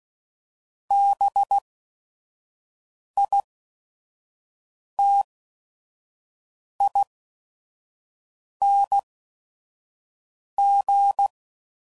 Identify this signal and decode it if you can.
Morse code, heard as a keyed tone, about 16 wpm, spaced as for 5 wpm, 800 Hz. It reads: BITING
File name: Suffixes / Special ING's / VerySlowToFaster ING's